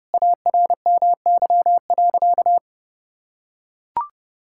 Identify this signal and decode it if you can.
Morse code, heard as a keyed tone, about 32 wpm, 700 Hz. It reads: ARMY.
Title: Sentences from Top 1000 Words - Review Part 6 - 30wpm